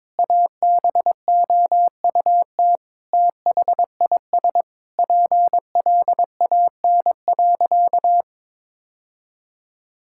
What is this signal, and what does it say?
Morse code, heard as a keyed tone, about 22 wpm, 700 Hz. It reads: ABOUT THIS PLAN.
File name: Sentences with 9-Word Phrases - 22wpm